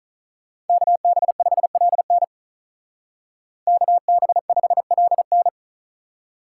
Morse code, heard as a keyed tone, about 46 wpm, 700 Hz. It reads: KB5LN KB5LN